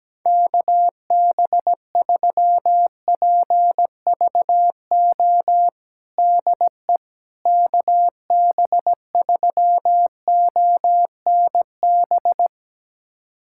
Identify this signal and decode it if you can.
Morse code, heard as a keyed tone, about 17 wpm, 700 Hz. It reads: KB3PVO DE KB3ONB
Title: USA Callsign DE Callsign - 6 Characters - 17wpm